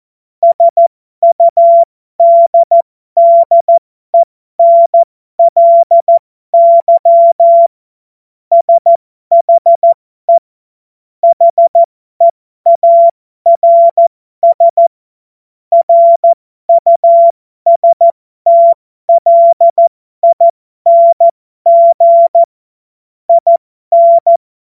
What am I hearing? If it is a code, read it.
Morse code, heard as a keyed tone, about 14 wpm, spaced as for 12 wpm, 680 Hz. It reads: SUDDENLY SHE HEARS RUSTLING IN
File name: Morsekurs / Witze